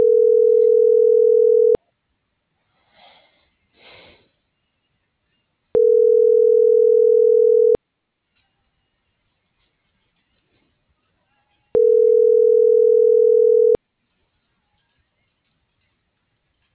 Ringback tone (North America), 440 + 480 Hz.